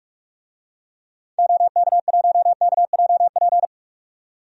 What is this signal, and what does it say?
Morse code, heard as a keyed tone, about 41 wpm, 700 Hz. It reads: OK1KJP
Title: DXCC-100 Call Signs - 6 Characters - 45wpm